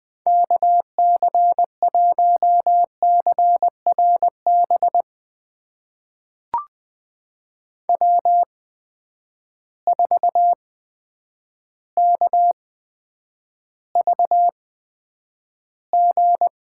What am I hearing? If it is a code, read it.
Morse code, heard as a keyed tone, about 20 wpm, spaced as for 9 wpm, 700 Hz. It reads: KC1CRB W4KVG
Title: USA Call Signs - 8x Character Spacing - 20wpm